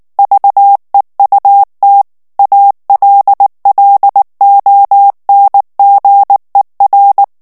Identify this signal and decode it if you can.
Morse code, heard as a keyed tone, about 19 wpm, 800 Hz. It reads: VEUT ALLONGER
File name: morse mp3 / FRANCAIS TEXTE CLAIR 1 VITESSE 18 MOTS MINUTES